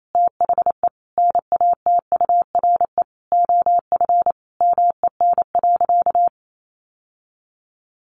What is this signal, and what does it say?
Morse code, heard as a keyed tone, about 28 wpm, 700 Hz. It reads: THE NATURE OF MEN.
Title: Sentences from Top 1500 Words 28wpm